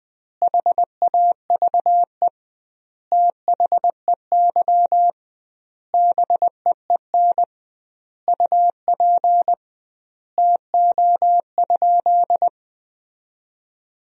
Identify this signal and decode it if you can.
Morse code, heard as a keyed tone, about 20 wpm, 700 Hz. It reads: HAVE THEY BEEN UP TO?
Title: 2x Word-Spacing - Sentences from Top 100 Words - 20wpm